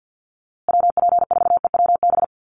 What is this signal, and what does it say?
Morse code, heard as a keyed tone, about 50 wpm, 700 Hz. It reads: WP4EWB
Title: USA Call Signs Repeated 2x 50wpm